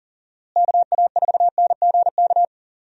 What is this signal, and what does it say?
Morse code, heard as a keyed tone, about 40 wpm, 700 Hz. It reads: KA4NGK
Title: USA Call Signs - 6 Characters - Repeated 3x - 40wpm